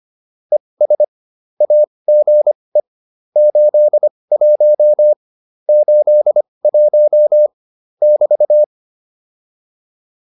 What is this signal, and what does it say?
Morse code, heard as a keyed tone, about 25 wpm, spaced as for 20 wpm, 600 Hz. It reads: ES AGE 81 81 <BT>
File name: QSO401a 20